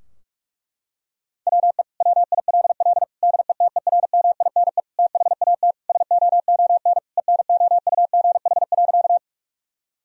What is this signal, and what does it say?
Morse code, heard as a keyed tone, about 46 wpm, 700 Hz. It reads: WE WILL DETERMINE THAT SOON ENOUGH.
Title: Sentences from Top 1000 Words - Review Part 4 - 45wpm